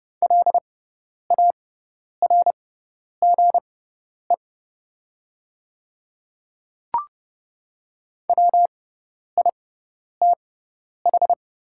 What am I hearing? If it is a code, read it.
Morse code, heard as a keyed tone, about 30 wpm, spaced as for 10 wpm, 690 Hz. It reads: LARGE WITH
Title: 6x Character Spacing - Top 300 Words - 30wpm